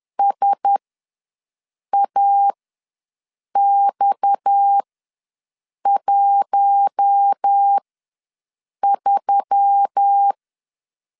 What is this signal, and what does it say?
Morse code, heard as a keyed tone, about 10 wpm, spaced as for 6 wpm, 800 Hz. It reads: SAX13
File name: LEZIONI / 9 286 ws 10wpm 4032